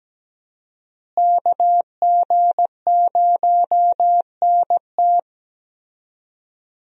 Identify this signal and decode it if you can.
Morse code, heard as a keyed tone, about 17 wpm, 700 Hz. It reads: KG0NT
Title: USA Call Signs - 5 Characters - Repeated 2x - 17wpm